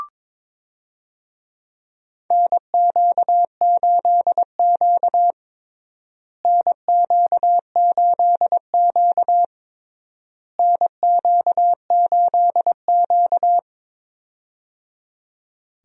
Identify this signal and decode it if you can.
Morse code, heard as a keyed tone, about 22 wpm, 700 Hz. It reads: NQ8Q NQ8Q NQ8Q
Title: USA Call Signs - 4 Characters - Repeated 3x - 22wpm